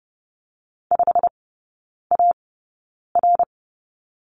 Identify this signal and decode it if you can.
Morse code, heard as a keyed tone, about 30 wpm, spaced as for 9 wpm, 700 Hz. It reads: 5AR
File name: course / 30:10wpm / 087 Lesson 15e - Number 5 - Three Characters 30wpm